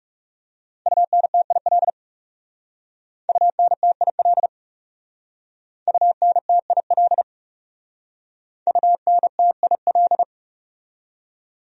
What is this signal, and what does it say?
Morse code, heard as a keyed tone, about 38 wpm, 700 Hz. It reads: UNTIL UNTIL UNTIL UNTIL